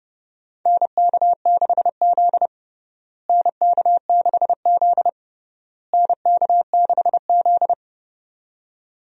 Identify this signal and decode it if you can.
Morse code, heard as a keyed tone, about 30 wpm, 700 Hz. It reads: NK6Z NK6Z NK6Z
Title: USA Call Signs - 4 Characters - Repeated 3x - 30wpm